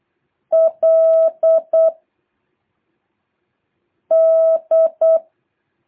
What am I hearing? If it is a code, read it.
Morse code, heard as a keyed tone, about 8 wpm, 660 Hz. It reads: L D
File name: jh8cbh / 2023cw